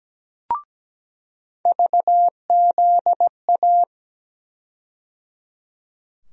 Morse code, heard as a keyed tone, about 17 wpm, 700 Hz. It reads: VZA
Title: USA Call Sign Postfixes - 3 Letters 17wpm